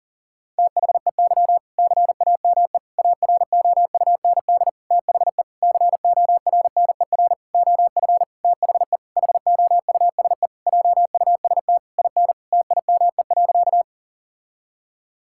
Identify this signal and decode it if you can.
Morse code, heard as a keyed tone, about 40 wpm, 700 Hz. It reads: THEY CAME AROUND THE CORNER OF THE HOUSE JUST IN TIME.
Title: Sentences from Top 1600 Words 40wpm